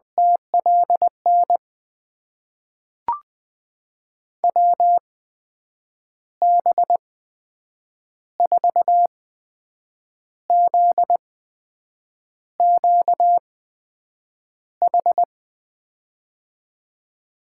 Morse code, heard as a keyed tone, about 20 wpm, spaced as for 7 wpm, 700 Hz. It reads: TLN WB4ZQH
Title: USA Call Signs - 8x Character Spacing - 20wpm